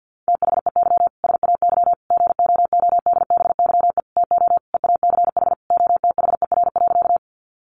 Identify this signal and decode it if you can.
Morse code, heard as a keyed tone, about 50 wpm, 700 Hz. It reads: THEY SAY GOODBYE TO EACH OTHER.
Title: Additional Easy Phrases 50wpm